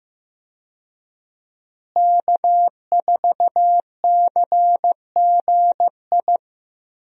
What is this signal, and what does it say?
Morse code, heard as a keyed tone, about 15 wpm, 700 Hz. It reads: K4CGI
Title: USA Call Signs - 5 Characters - Repeated 3x - 15wpm